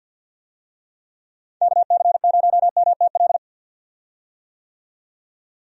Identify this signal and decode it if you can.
Morse code, heard as a keyed tone, about 52 wpm, 700 Hz. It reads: KC0MTL
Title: USA Call Signs - 6 Characters - 50wpm